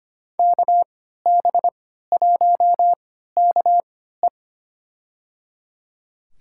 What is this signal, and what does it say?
Morse code, heard as a keyed tone, about 25 wpm, spaced as for 14 wpm, 700 Hz. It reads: KB1KE
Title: USA Call Signs - 5 Characters - 3x Character Spacing - 25wpm